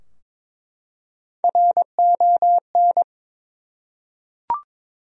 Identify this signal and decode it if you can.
Morse code, heard as a keyed tone, about 22 wpm, 700 Hz. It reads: RON
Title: USA Common Names - Max 4 Letters - 22wpm